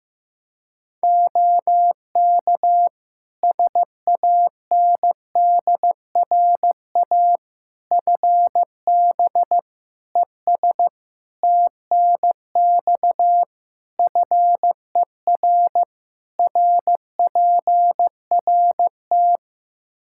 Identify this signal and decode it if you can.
Morse code, heard as a keyed tone, about 15 wpm, 700 Hz. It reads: OK SANDRA FB ES TNX FER RPRT
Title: Practice QSO - Exchange 2 - Info - 15wpm